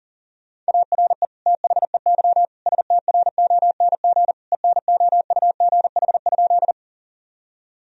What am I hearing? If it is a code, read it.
Morse code, heard as a keyed tone, about 40 wpm, 700 Hz. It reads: ARE THEY STRONG ENOUGH?
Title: Sentences from Top 1000 Words - Review Part 3 - 40wpm